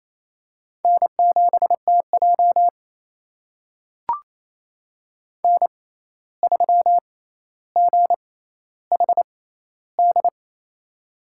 Morse code, heard as a keyed tone, about 28 wpm, spaced as for 13 wpm, 700 Hz. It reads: N7TJ N3GHD
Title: USA Call Signs - 6x Character Spacing - 28wpm